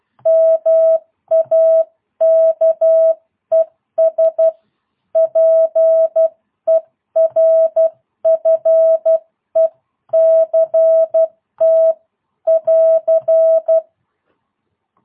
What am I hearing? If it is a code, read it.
Morse code, heard as a keyed tone, about 12 wpm, 650 Hz. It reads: MAKES PERFECT <AR>
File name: jh8cbh / 2022cw